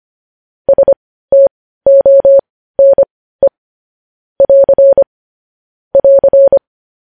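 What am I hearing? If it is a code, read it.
Morse code, heard as a keyed tone, about 25 wpm, spaced as for 15 wpm, 550 Hz. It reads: STONE <AR> <AR>
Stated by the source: Morse practice of news headlines at 15 wpm.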